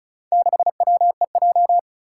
Morse code, heard as a keyed tone, about 36 wpm, 700 Hz. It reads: 6WEJ
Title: Four Challenging Letter-Number Combinations 35wpm